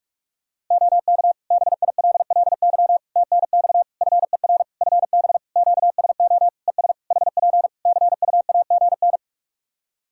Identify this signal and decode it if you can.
Morse code, heard as a keyed tone, about 46 wpm, 700 Hz. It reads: OK BILLY TNX FER FB QSO ES HP CUAGN
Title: Practice QSO - Exchange 4 - Ending - 45wpm